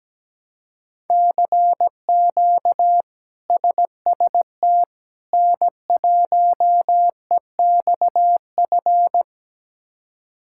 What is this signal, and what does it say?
Morse code, heard as a keyed tone, about 17 wpm, 700 Hz. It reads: CQ SST N1EXF
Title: K1USN SST QSO Practice - 17wpm